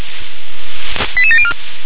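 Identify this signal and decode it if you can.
Five-tone selective call (ZVEI), heard as quick runs of five tones, tones of about 70 ms. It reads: 88704
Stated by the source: ZVEI five-tone selective call received by SDR.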